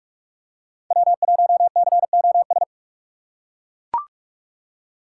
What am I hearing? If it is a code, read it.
Morse code, heard as a keyed tone, about 41 wpm, 700 Hz. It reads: W1COS
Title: USA Call Signs - 5 Characters - 2x Character Spacing - 45wpm